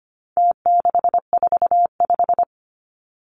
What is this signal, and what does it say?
Morse code, heard as a keyed tone, about 25 wpm, 700 Hz. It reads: T645
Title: Numbers - Contest Style - 4-Digit 25wpm